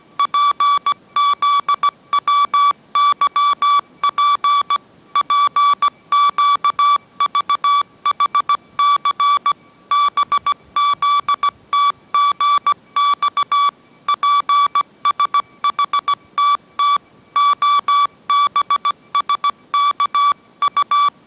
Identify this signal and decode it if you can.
Morse code, heard as a keyed tone, about 17 wpm, 1150 Hz. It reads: PZWYP PQVHC BZTGX PSHTT OBSKU